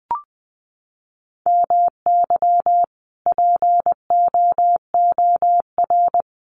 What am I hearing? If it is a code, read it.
Morse code, heard as a keyed tone, about 20 wpm, 700 Hz. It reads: MY POOR
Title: Sentences from Top 1000 Words 20wpm